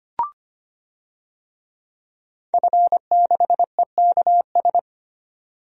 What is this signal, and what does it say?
Morse code, heard as a keyed tone, about 25 wpm, 690 Hz. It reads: F6EKS
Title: DXCC-100 Call Signs - 5 Characters - 25wpm